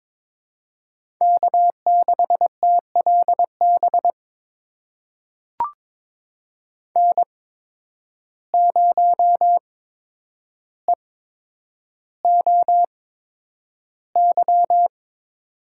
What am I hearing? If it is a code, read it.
Morse code, heard as a keyed tone, about 22 wpm, spaced as for 9 wpm, 700 Hz. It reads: K6TLB N0EOY
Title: USA Call Signs - 5 Characters - 8x Character Spacing - 22wpm